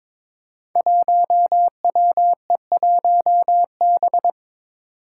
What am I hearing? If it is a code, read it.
Morse code, heard as a keyed tone, about 22 wpm, 700 Hz. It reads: 1WE1B
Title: Five Challenging Letter-Number Combinations 22wpm